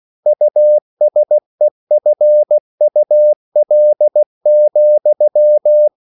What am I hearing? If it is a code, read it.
Morse code, heard as a keyed tone, about 16 wpm, 600 Hz. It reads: USEFUL,